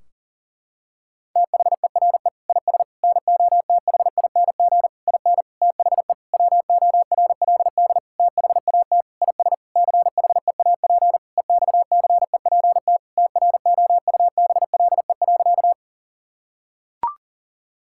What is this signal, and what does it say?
Morse code, heard as a keyed tone, about 40 wpm, 700 Hz. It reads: THERE IS NOTHING IN THE WORLD THAT IS CHEAP EXCEPT TROUBLE.
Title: Sentences from Top 1600 Words 40wpm